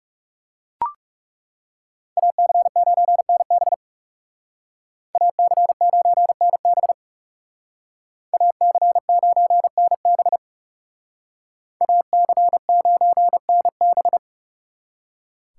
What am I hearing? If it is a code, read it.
Morse code, heard as a keyed tone, about 38 wpm, 700 Hz. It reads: AC9NB AC9NB AC9NB AC9NB